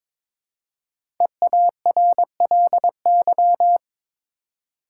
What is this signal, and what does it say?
Morse code, heard as a keyed tone, about 22 wpm, 700 Hz. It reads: EARLY
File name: 5 Letter Words - First 3 Letters Given - 22wpm